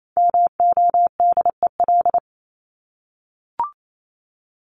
Morse code, heard as a keyed tone, about 28 wpm, 700 Hz. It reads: MODEL
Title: Top 501-750 Words 28wpm